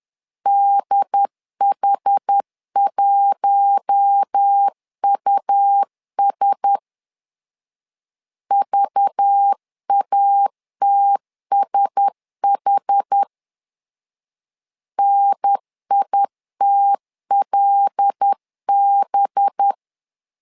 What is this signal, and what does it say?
Morse code, heard as a keyed tone, about 10 wpm, 800 Hz. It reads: DH1US VATSH NITLB
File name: LEZIONI / 5 LBVC / 5 LBV 10wpm 2248